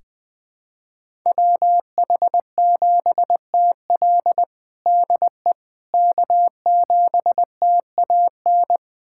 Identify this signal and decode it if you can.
Morse code, heard as a keyed tone, about 20 wpm, 700 Hz. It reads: WH7TL DE K7TAN
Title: USA Callsign DE Callsign - 5 Characters - 20wpm